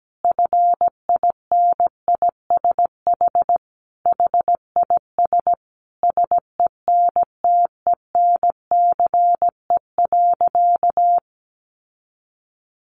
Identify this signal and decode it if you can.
Morse code, heard as a keyed tone, about 17 wpm, 700 Hz. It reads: FINISH HIS SENTENCE.
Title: Sentences from Top 1200 Words 17wpm